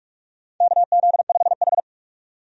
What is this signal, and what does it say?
Morse code, heard as a keyed tone, about 46 wpm, 700 Hz. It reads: KZ5H
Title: USA Call Signs - 4 Characters - 45wpm